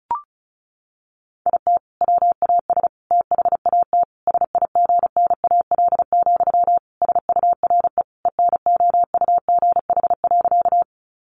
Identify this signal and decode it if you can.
Morse code, heard as a keyed tone, about 34 wpm, 700 Hz. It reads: IT WAS THAT SIGNAL, SURE ENOUGH.
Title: Sentences from Top 1400 Words 35wpm